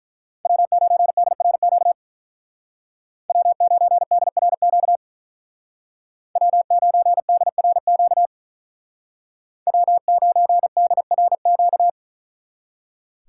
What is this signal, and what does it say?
Morse code, heard as a keyed tone, about 43 wpm, 700 Hz. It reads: W9DRQ W9DRQ W9DRQ W9DRQ
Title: Speed Racing - USA 1x3 Call Signs 35wpm